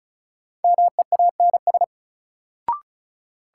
Morse code, heard as a keyed tone, about 36 wpm, 700 Hz. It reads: MEANS
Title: Sets of 5 Words - New Word Drill - 35wpm